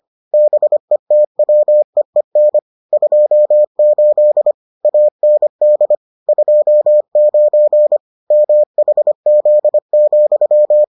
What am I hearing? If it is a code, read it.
Morse code, heard as a keyed tone, about 25 wpm, 600 Hz. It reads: BETWEEN 28 AND 29 MHZ,